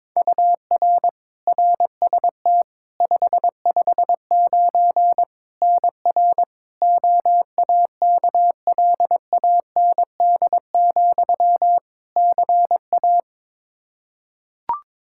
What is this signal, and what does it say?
Morse code, heard as a keyed tone, about 22 wpm, 700 Hz. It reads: UR RST 559 NR OAKLAND, CA
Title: K1USN SST QSO Practice 22wpm